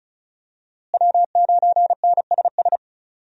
Morse code, heard as a keyed tone, about 36 wpm, 700 Hz. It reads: W9NSS
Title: USA Call Signs - 5 Characters - Repeated 3x - 35wpm